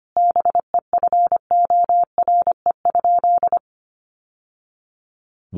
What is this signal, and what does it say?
Morse code, heard as a keyed tone, about 25 wpm, 700 Hz. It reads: BEFORE?
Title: Sentences from Top 900 Words 25wpm